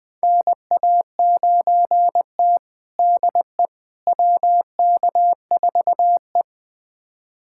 Morse code, heard as a keyed tone, about 21 wpm, 700 Hz. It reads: NA9T DE WK4E
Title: USA Callsign DE Callsign - 4 Characters - 20wpm